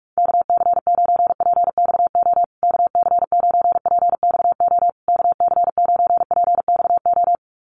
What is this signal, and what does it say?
Morse code, heard as a keyed tone, about 44 wpm, 700 Hz. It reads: KC9PXO KC9PXO KC9PXO
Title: USA Call Signs Repeated 3x 45wpm